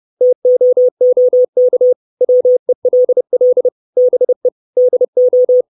Morse code, heard as a keyed tone, about 30 wpm, 500 Hz. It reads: TOOK WELL BE DO
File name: 632 Words 30WPM